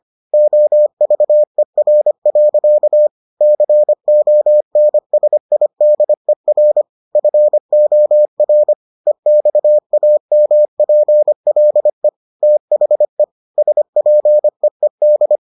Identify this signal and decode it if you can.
Morse code, heard as a keyed tone, about 25 wpm, 600 Hz. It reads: OVER. CONSIDER FOR EXAMPLE THE SPEED